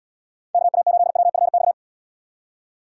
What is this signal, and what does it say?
Morse code, heard as a keyed tone, about 73 wpm, 700 Hz. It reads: KA8WFX